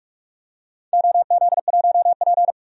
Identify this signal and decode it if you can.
Morse code, heard as a keyed tone, about 44 wpm, 700 Hz. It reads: OZ1P
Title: DXCC-100 Call Signs - 4 Characters - 45wpm